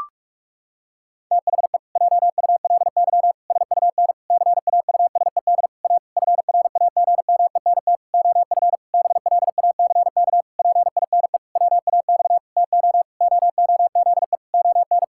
Morse code, heard as a keyed tone, about 46 wpm, 700 Hz. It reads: THE JULY SUN CAUSED A FRAGMENT OF BLACK PINE WAX TO OOZE ON
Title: Pangram Challenge v2 - 45wpm